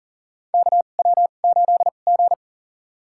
Morse code, heard as a keyed tone, about 40 wpm, spaced as for 29 wpm, 700 Hz. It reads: KW8G
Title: USA Call Signs - 4 Characters - 2x Character Spacing - 40wpm